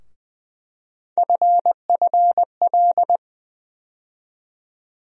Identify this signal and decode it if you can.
Morse code, heard as a keyed tone, about 20 wpm, 700 Hz. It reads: FFL
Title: USA Call Sign Postfixes - 3 Letters 20wpm